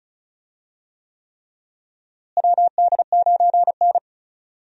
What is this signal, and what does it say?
Morse code, heard as a keyed tone, about 36 wpm, 700 Hz. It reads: WD9N
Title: USA Call Signs - 4 Characters - Repeated 3x - 35wpm